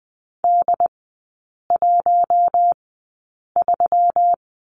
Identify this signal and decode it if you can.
Morse code, heard as a keyed tone, about 20 wpm, spaced as for 8 wpm, 700 Hz. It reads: D13
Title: 212 Lesson 34e - Number 8 - Three Characters 20wpm